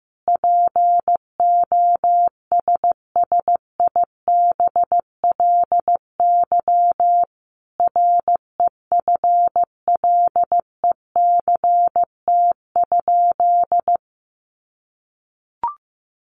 Morse code, heard as a keyed tone, about 15 wpm, 700 Hz. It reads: POSSIBLY REFLECT?
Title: Sentences from Top 1500 Words 15wpm